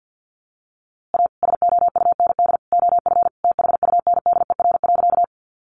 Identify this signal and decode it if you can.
Morse code, heard as a keyed tone, about 50 wpm, 700 Hz. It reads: A SOUND OF THUNDER.
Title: A Sound Of Thunder 50wpm